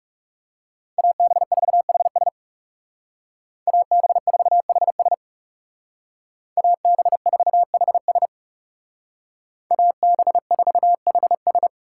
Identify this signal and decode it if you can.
Morse code, heard as a keyed tone, about 37 wpm, 700 Hz. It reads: AB4HS AB4HS AB4HS AB4HS
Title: Speed Racing - USA 2x2 Call Signs 30wpm